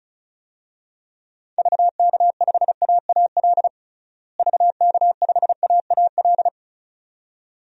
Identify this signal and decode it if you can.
Morse code, heard as a keyed tone, about 34 wpm, 700 Hz. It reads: VK5AAL VK5AAL
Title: DXCC-100 Call Signs - 6 Characters - Repeated 2x - 35wpm